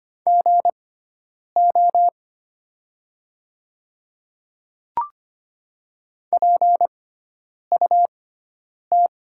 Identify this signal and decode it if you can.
Morse code, heard as a keyed tone, about 25 wpm, spaced as for 9 wpm, 700 Hz. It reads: GO PUT
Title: 6x Character Spacing - Top 300 Words - 25wpm